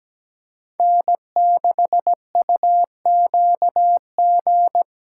Morse code, heard as a keyed tone, about 17 wpm, 700 Hz. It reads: N6UQG
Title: USA Callsign DE Callsign - 5 Characters - 17wpm